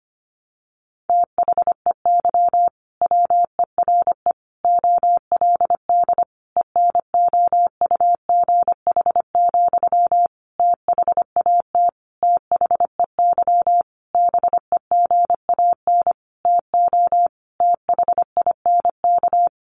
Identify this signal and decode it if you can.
Morse code, heard as a keyed tone, about 25 wpm, 700 Hz. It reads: THEY WERE OLD ENOUGH, THAT THEY BEGAN TO THINK
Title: Sentences from Top 800 Words 25wpm